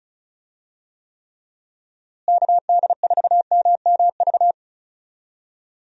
Morse code, heard as a keyed tone, about 34 wpm, 700 Hz. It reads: KD4MMV
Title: USA Call Signs - 6 Characters - Repeated 3x - 35wpm